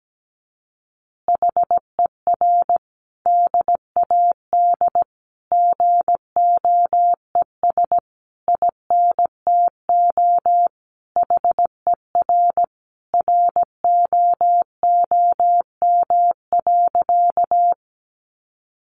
Morse code, heard as a keyed tone, about 17 wpm, 700 Hz. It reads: HER DAD GOES INTO HER ROOM.
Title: Additional Easy Phrases 17wpm